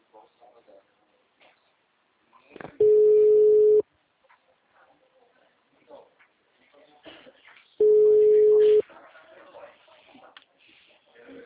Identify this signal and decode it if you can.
Ringback tone (Europe, 425 Hz), with speech.
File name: out- -2014- - - .77966